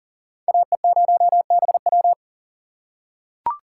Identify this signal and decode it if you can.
Morse code, heard as a keyed tone, about 40 wpm, 700 Hz. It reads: AE0BW
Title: USA Call Signs - 6x Character Spacing - 40wpm